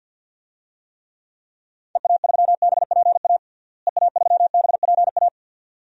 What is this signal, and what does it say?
Morse code, heard as a keyed tone, about 52 wpm, 700 Hz. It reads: EA3BPA EA3BPA
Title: DXCC-100 Call Signs - 6 Characters - Repeated 2x - 50wpm